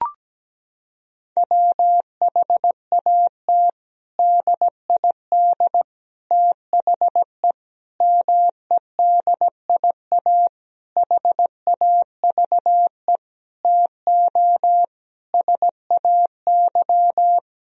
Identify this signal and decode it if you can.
Morse code, heard as a keyed tone, about 17 wpm, 700 Hz. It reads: WHAT DID THE MEDIA HAVE TO SAY
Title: Sentences from Top 1000 Words - Review Part 4 - 17wpm